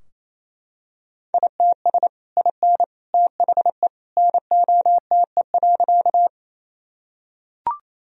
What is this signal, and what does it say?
Morse code, heard as a keyed tone, about 27 wpm, 700 Hz. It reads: ITS IN THE NOTE.
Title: Sentences from Top 1000 Words - Review Part 2 - 28wpm